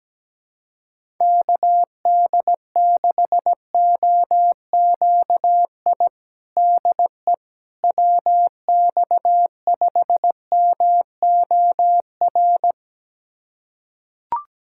Morse code, heard as a keyed tone, about 17 wpm, 700 Hz. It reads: KD6OQI DE WX5MOR
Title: USA Callsign DE Callsign - 6 Characters - 17wpm